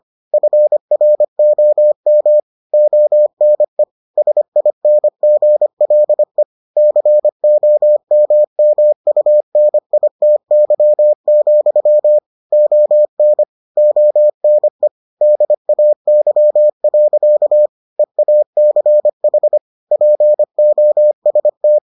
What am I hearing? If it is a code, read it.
Morse code, heard as a keyed tone, about 25 wpm, 600 Hz. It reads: FROM ONE SINGLE COMMUNITY, ON ONE DAY. EACH POST